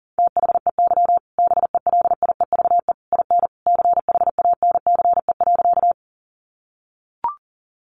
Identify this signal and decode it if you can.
Morse code, heard as a keyed tone, about 40 wpm, 700 Hz. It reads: THEY BELIEVE IN CHANCE.
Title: Sentences from Top 1000 Words 40wpm